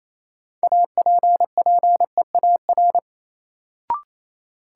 Morse code, heard as a keyed tone, about 28 wpm, 700 Hz. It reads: APPEAR
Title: Sentences from Top 1200 Words - New Word Drill Alt - 28wpm